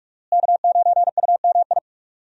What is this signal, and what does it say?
Morse code, heard as a keyed tone, about 46 wpm, 700 Hz. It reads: K9UMI
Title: USA Call Signs - 5 Characters - Repeated 3x - 45wpm